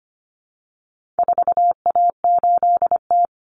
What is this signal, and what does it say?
Morse code, heard as a keyed tone, about 25 wpm, 700 Hz. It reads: 4A8T
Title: Common Cut Numbers - 4-Digit 25wpm